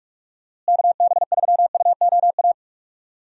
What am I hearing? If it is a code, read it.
Morse code, heard as a keyed tone, about 46 wpm, 700 Hz. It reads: KB3UOA
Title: USA Call Signs - 6 Characters - Repeated 2x - 45wpm